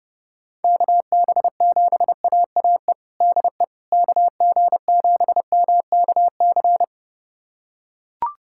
Morse code, heard as a keyed tone, about 30 wpm, 700 Hz. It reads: KB7AAE DE KG7MKC
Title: USA Callsign DE Callsign - 6 Characters - 30wpm